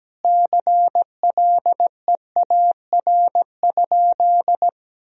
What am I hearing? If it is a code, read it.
Morse code, heard as a keyed tone, about 17 wpm, 700 Hz. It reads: CLEAR?
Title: Sentences from Top 1000 Words - Review Part 4 - 17wpm